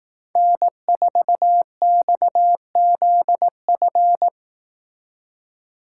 Morse code, heard as a keyed tone, about 18 wpm, 700 Hz. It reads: N4XZF
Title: Speed Racing - USA 1x3 Call Signs 15wpm